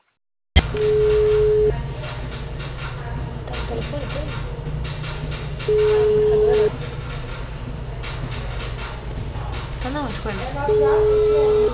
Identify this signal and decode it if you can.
Ringback tone (Europe, 425 Hz), with speech.